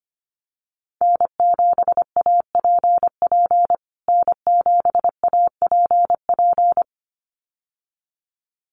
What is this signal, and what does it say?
Morse code, heard as a keyed tone, about 25 wpm, 700 Hz. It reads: N7APP N7APP
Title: USA Call Signs Repeated 2x 25wpm